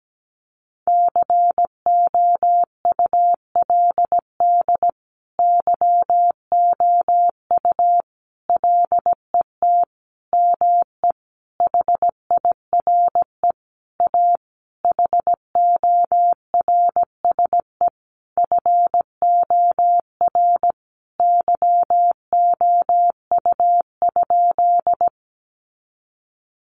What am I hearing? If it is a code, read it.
Morse code, heard as a keyed tone, about 17 wpm, 700 Hz. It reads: COULD YOU LET ME HIRE A HORSE FOR YOU?
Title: Sentences from Top 1400 Words 17wpm